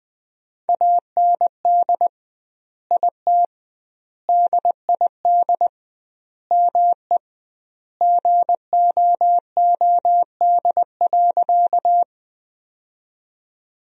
Morse code, heard as a keyed tone, about 20 wpm, 700 Hz. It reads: AND IT DID ME GOOD.
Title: 2x Word-Spacing - Sentences from Top 200 Words - 20wpm